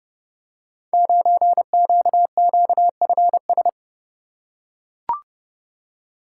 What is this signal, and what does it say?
Morse code, heard as a keyed tone, about 30 wpm, 700 Hz. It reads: 9QQFS
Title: Code Groups - 5 Letter:Numbers - 30wpm